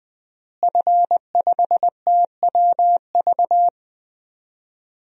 Morse code, heard as a keyed tone, about 21 wpm, 700 Hz. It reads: F5TWV
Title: DXCC-100 Call Signs - 5 Characters - Repeated 3x - 20wpm